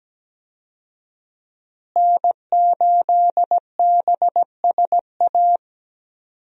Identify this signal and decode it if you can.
Morse code, heard as a keyed tone, about 17 wpm, 700 Hz. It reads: N8BSA
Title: USA Call Signs - 5 Characters - Repeated 3x - 17wpm